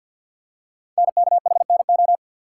Morse code, heard as a keyed tone, about 47 wpm, 700 Hz. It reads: NKHNO
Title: Code Groups - 5 Letters - 50wpm